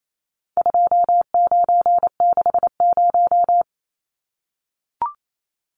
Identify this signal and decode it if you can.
Morse code, heard as a keyed tone, about 28 wpm, 700 Hz. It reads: 2960
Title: Numbers - 4-Digit 28wpm